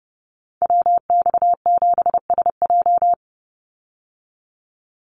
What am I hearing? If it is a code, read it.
Morse code, heard as a keyed tone, about 30 wpm, 700 Hz. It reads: WX7SJ
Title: Call Signs Repeated 2x - 30wpm